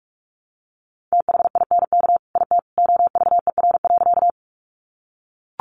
Morse code, heard as a keyed tone, about 44 wpm, 700 Hz. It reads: THINK IT OVER.
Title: Sentences from Top 200 Words 45wpm